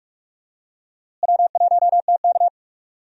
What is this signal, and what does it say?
Morse code, heard as a keyed tone, about 44 wpm, 700 Hz. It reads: W1TK
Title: USA Call Signs - 4 Characters - Repeated 2x - 45wpm